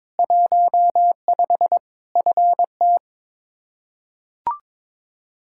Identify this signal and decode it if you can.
Morse code, heard as a keyed tone, about 22 wpm, 700 Hz. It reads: 15 FT
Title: Practice QSO - Exchange 2 - Info - 22wpm